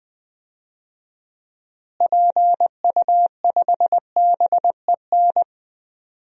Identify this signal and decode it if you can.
Morse code, heard as a keyed tone, about 20 wpm, 700 Hz. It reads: PU5BEN
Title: DXCC-100 Call Signs - 6 Characters - Repeated 3x - 20wpm